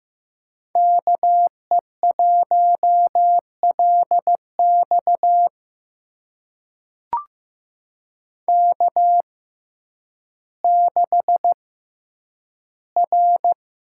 Morse code, heard as a keyed tone, about 15 wpm, spaced as for 9 wpm, 700 Hz. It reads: KE1LX K6R
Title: USA Call Signs - 6x Character Spacing - 15wpm